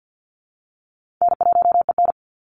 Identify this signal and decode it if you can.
Morse code, heard as a keyed tone, about 50 wpm, 700 Hz. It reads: N1EN
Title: Easy Call Signs - 50wpm